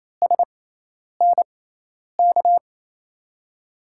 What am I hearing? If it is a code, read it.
Morse code, heard as a keyed tone, about 28 wpm, spaced as for 10 wpm, 700 Hz. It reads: SNK
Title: USA Call Signs - 6x Character Spacing - 28wpm